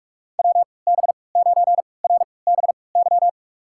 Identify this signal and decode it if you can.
Morse code, heard as a keyed tone, about 46 wpm, spaced as for 25 wpm, 700 Hz. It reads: WB9RBY
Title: USA Call Signs - 3x Character Spacing - 45wpm